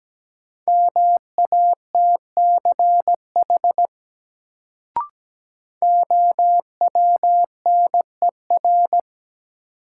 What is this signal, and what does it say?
Morse code, heard as a keyed tone, about 17 wpm, 700 Hz. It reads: MATCH OWNER
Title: Sentences from Top 1600 Words - New Word Drill Alt - 17wpm